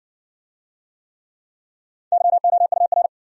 Morse code, heard as a keyed tone, about 60 wpm, 700 Hz. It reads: YOUR